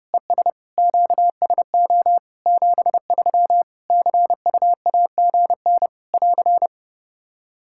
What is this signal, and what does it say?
Morse code, heard as a keyed tone, about 30 wpm, 700 Hz. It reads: ES QSO 73 CUAGN <AR>
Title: Practice QSO - Exchange 4 - Ending - 30wpm